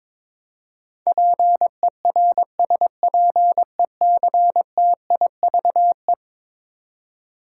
Morse code, heard as a keyed tone, about 22 wpm, 700 Hz. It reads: PERSPECTIVE
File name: practice / 22wpm / Eleven Letter Words - 22wpm